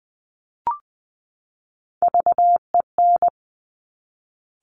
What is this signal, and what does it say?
Morse code, heard as a keyed tone, about 21 wpm, 700 Hz. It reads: VEN
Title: Top Three Letter Combinations in Words 20wpm